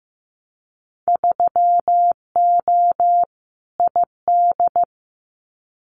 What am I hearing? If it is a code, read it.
Morse code, heard as a keyed tone, about 15 wpm, 700 Hz. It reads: 3O ID
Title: Winter Field Day Exchanges 15wpm